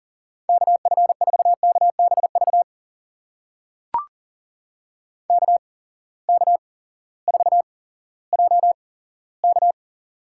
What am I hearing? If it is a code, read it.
Morse code, heard as a keyed tone, about 40 wpm, spaced as for 18 wpm, 700 Hz. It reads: KF4KBV KK4JK
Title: USA Call Signs - 6 Characters - 8x Character Spacing - 40wpm